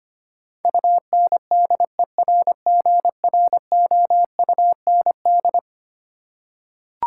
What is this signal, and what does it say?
Morse code, heard as a keyed tone, about 25 wpm, 700 Hz. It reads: UNDERGROUND
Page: morse-code-ninja-practice-25wpm